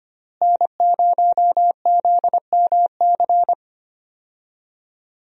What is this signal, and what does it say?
Morse code, heard as a keyed tone, about 25 wpm, 700 Hz. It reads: N0ZMC
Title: USA Call Signs - 5 Characters - Repeated 3x - 25wpm